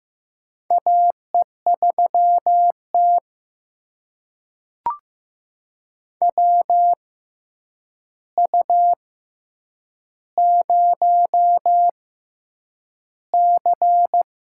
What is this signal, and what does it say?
Morse code, heard as a keyed tone, about 15 wpm, spaced as for 8 wpm, 700 Hz. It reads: AE3T WU0C
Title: USA Call Signs - 4 Characters - 6x Character Spacing - 15wpm